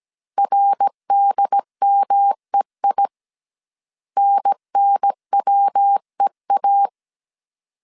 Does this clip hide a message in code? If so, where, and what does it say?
Yes: Morse code from 0.4 to 6.9 s, heard as a keyed tone, about 17 wpm, 800 Hz. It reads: RDMEI NNWEA